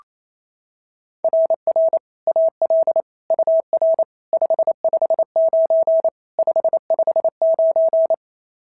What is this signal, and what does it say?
Morse code, heard as a keyed tone, about 25 wpm, 650 Hz. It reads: RR AL UR 559 559
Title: SOTA QSO Practice 28wpm